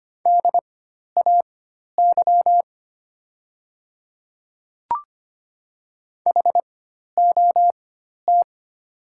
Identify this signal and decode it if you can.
Morse code, heard as a keyed tone, about 25 wpm, spaced as for 12 wpm, 700 Hz. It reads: DAY HOT
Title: 4x Character Spacing - Top 200 Words - 25wpm